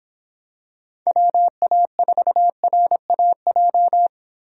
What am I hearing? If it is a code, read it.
Morse code, heard as a keyed tone, about 25 wpm, 700 Hz. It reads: WA4RAJ